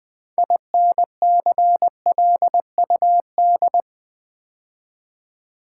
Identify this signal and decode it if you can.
Morse code, heard as a keyed tone, about 20 wpm, 700 Hz. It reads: INCLUD
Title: Build A Word - 3 Letters At A Time - 9-Letter Words - 20wpm